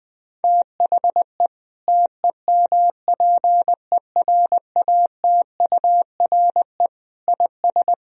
Morse code, heard as a keyed tone, about 20 wpm, 700 Hz. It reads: THE TEMPERATURE IS
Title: Sentences from Top 1000 Words - Review Part 4 - 20wpm